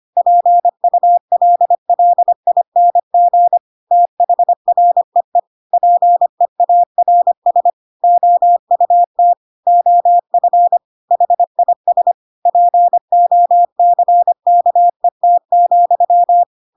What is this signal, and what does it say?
Morse code, heard as a keyed tone, about 25 wpm, 700 Hz. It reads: PULLING THREE PEARS OUT OF HIS POCKET,
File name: Pinocchio / Pinocchio ch7 25 25 ik1hge